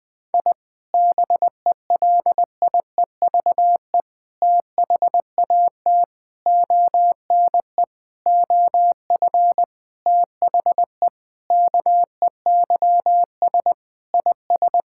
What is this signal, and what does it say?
Morse code, heard as a keyed tone, about 21 wpm, 700 Hz. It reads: I BELIEVE THAT ONE OF THE KEYS IS
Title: Sentences with 4-Word Phrases - Spoken Beginning - 20wpm